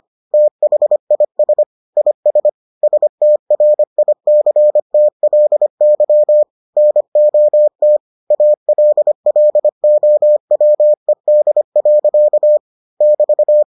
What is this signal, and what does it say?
Morse code, heard as a keyed tone, about 25 wpm, 600 Hz. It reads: THIS IS STRICTLY NOT ALLOWED. <BT>